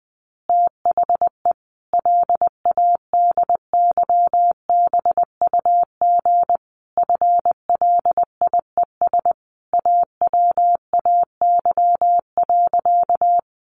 Morse code, heard as a keyed tone, about 20 wpm, 700 Hz. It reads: THE LADYBUG FLIES AWAY.
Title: Additional Easy Phrases 20wpm